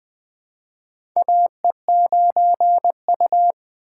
Morse code, heard as a keyed tone, about 20 wpm, 700 Hz. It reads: AE9U
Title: USA Call Signs - 4 Characters - 4x Character Spacing - 20wpm